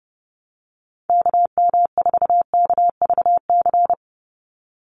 Morse code, heard as a keyed tone, about 30 wpm, 700 Hz. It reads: KM4KVC
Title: USA Call Signs 30wpm